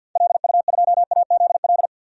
Morse code, heard as a keyed tone, about 52 wpm, 700 Hz. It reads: LU2AZL
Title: DXCC-100 Call Signs - 6 Characters - 50wpm